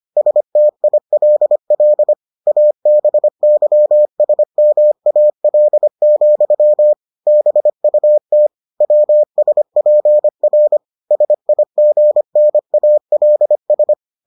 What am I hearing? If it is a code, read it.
Morse code, heard as a keyed tone, about 25 wpm, 600 Hz. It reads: STILL ABYSMAL, BUT WSPR SIGNALS